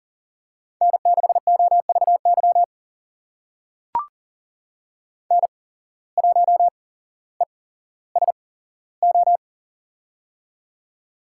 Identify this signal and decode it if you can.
Morse code, heard as a keyed tone, about 39 wpm, spaced as for 17 wpm, 700 Hz. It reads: N6OVY N1ESO
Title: USA Call Signs - 5 Characters - 8x Character Spacing - 40wpm